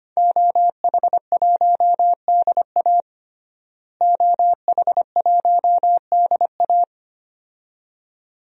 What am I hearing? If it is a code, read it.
Morse code, heard as a keyed tone, about 25 wpm, 700 Hz. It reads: OH1DA OH1DA
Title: DXCC-100 Call Signs - 5 Characters - Repeated 2x - 25wpm